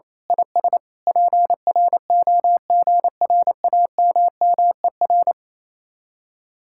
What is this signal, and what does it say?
Morse code, heard as a keyed tone, about 26 wpm, 700 Hz. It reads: IS PROGRAMMER
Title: Speed-Racing Common QSO Phrases - 28wpm